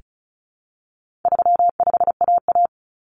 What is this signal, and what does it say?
Morse code, heard as a keyed tone, about 34 wpm, 700 Hz. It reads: 35AA
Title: Common Cut Numbers - 4-Digit 35wpm